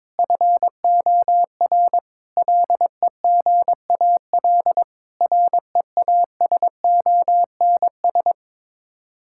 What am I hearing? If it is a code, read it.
Morse code, heard as a keyed tone, about 22 wpm, 700 Hz. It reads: FOR LEGAL REASONS
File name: practice / 22wpm / Sets of 8 Words - 22wpm